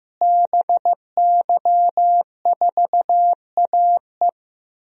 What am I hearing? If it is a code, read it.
Morse code, heard as a keyed tone, about 15 wpm, 700 Hz. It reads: BY4AE
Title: DXCC-100 Call Signs - 5 Characters - Repeated 2x - 15wpm